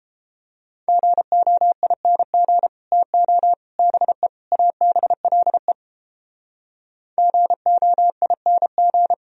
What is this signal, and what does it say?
Morse code, heard as a keyed tone, about 29 wpm, 700 Hz. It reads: GOING TO BE ABLE GOING